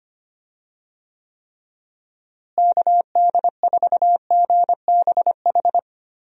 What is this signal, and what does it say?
Morse code, heard as a keyed tone, about 25 wpm, 700 Hz. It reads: KD4GBH
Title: USA Call Signs - 6 Characters - Repeated 2x - 25wpm